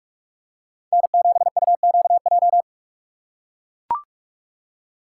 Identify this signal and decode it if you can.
Morse code, heard as a keyed tone, about 44 wpm, 700 Hz. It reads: N7UQJ